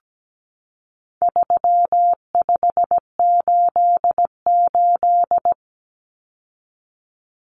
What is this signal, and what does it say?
Morse code, heard as a keyed tone, about 17 wpm, 700 Hz. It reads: 3588
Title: Common Cut Numbers - 4-Digit 17wpm